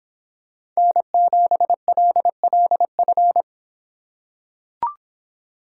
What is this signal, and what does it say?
Morse code, heard as a keyed tone, about 25 wpm, 700 Hz. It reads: N7LLF